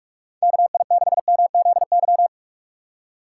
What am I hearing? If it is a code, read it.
Morse code, heard as a keyed tone, about 46 wpm, 700 Hz. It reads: KI6MZY
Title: USA Call Signs - 6 Characters - 3x Character Spacing - 45wpm